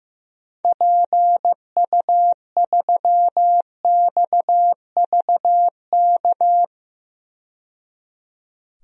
Morse code, heard as a keyed tone, about 15 wpm, 700 Hz. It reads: PU3XVK
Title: DXCC-100 Call Signs - 6 Characters - 15wpm